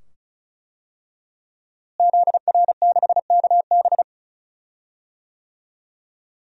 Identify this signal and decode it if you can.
Morse code, heard as a keyed tone, about 34 wpm, 700 Hz. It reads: ZR6KB
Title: DXCC-100 Call Signs - 5 Characters - Repeated 2x - 35wpm